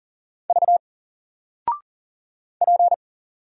Morse code, heard as a keyed tone, about 40 wpm, 700 Hz. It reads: V P
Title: Single Character (Letter:Number:Punctuation:Prosign) 40wpm